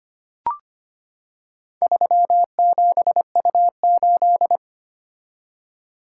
Morse code, heard as a keyed tone, about 25 wpm, 700 Hz. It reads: 37U8
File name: practice / 25wpm / Four Letter-Number Combinations - 25wpm